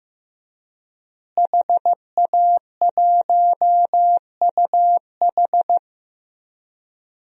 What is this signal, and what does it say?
Morse code, heard as a keyed tone, about 15 wpm, 700 Hz. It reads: HA1UH
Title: DXCC-100 Call Signs - 5 Characters - Repeated 3x - 15wpm